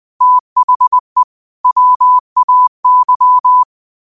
Morse code, heard as a keyed tone, about 20 wpm, 1000 Hz. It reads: THE WAY